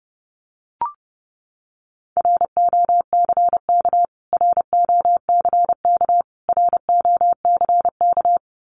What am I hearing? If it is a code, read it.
Morse code, heard as a keyed tone, about 30 wpm, 700 Hz. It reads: ROCK ROCK ROCK
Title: Top 500 Words Repeated 30wpm